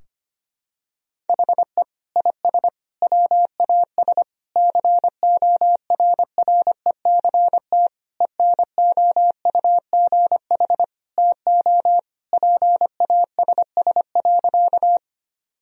Morse code, heard as a keyed tone, about 25 wpm, 700 Hz. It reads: HE IS WAS CORRECT ENOUGH TO PASS.
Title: Sentences from Top 1000 Words - Review Part 2 - 25wpm